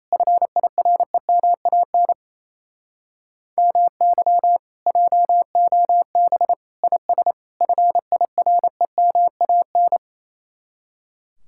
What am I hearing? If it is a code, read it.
Morse code, heard as a keyed tone, about 34 wpm, 700 Hz. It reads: FIREMAN MY JOB IS FIREMAN
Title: Speed-Racing Common QSO Phrases - 28wpm